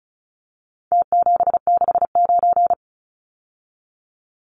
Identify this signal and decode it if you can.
Morse code, heard as a keyed tone, about 34 wpm, 700 Hz. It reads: T769
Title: Numbers - Contest Style - 4-Digit 35wpm